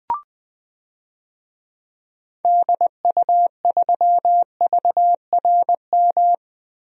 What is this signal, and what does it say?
Morse code, heard as a keyed tone, about 21 wpm, 700 Hz. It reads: DU3VRM
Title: DXCC-100 Call Signs - 6 Characters - Repeated 3x - 20wpm